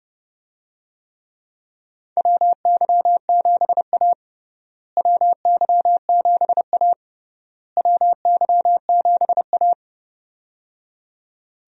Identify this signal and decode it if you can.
Morse code, heard as a keyed tone, about 30 wpm, 700 Hz. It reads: WY7A WY7A WY7A
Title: USA Call Signs - 4 Characters - Repeated 3x - 30wpm